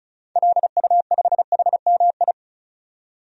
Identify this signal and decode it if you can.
Morse code, heard as a keyed tone, about 36 wpm, 700 Hz. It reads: LU5HMI
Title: DXCC-100 Call Signs - 6 Characters - 35wpm